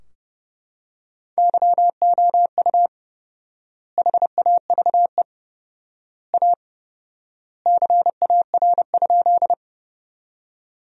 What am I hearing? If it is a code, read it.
Morse code, heard as a keyed tone, about 30 wpm, 700 Hz. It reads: YOU HAVE A CAR?
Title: 4x Word-Spacing - Sentences from Top 300 Words - 30wpm